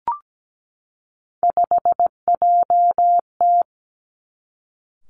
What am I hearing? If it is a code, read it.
Morse code, heard as a keyed tone, about 17 wpm, 700 Hz. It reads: 5JT
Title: Three Challenging Letter-Number Combinations 17wpm